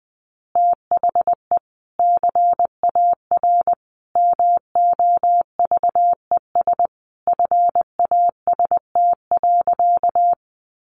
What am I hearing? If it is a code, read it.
Morse code, heard as a keyed tone, about 20 wpm, 700 Hz. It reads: THE CAR MOVES FAST.
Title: Easy Phrases 20wpm